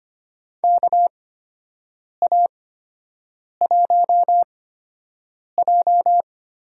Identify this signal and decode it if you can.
Morse code, heard as a keyed tone, about 25 wpm, spaced as for 7 wpm, 700 Hz. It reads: KA1J
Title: USA Call Signs - 4 Characters - 8x Character Spacing - 25wpm